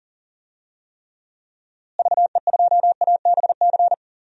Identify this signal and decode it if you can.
Morse code, heard as a keyed tone, about 37 wpm, 700 Hz. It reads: VE2ABC
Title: DXCC-100 Call Signs - 6 Characters - Repeated 3x - 40wpm